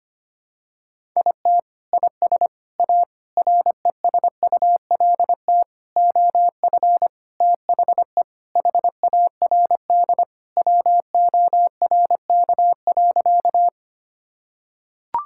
Morse code, heard as a keyed tone, about 25 wpm, 700 Hz. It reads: IT IS A RESULT OF THE HARD WORK.
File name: practice / 25wpm / Sentences with 4-Word Phrases - 25wpm